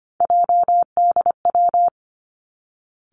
Morse code, heard as a keyed tone, about 25 wpm, 700 Hz. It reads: JDW
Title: Three Challenging Letter-Number Combinations 25wpm